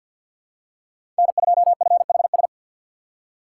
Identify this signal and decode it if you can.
Morse code, heard as a keyed tone, about 48 wpm, 700 Hz. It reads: N2FHS
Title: USA Call Signs - 5 Characters - 6x Character Spacing - 50wpm